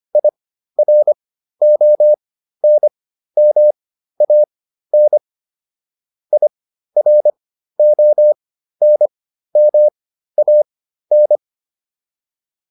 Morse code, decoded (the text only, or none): IRONMAN IRONMAN